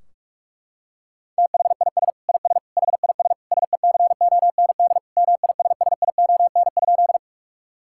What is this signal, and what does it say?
Morse code, heard as a keyed tone, about 46 wpm, 700 Hz. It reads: THIS IS HIS SECOND MISSION?
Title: Sentences from Top 1600 Words 45wpm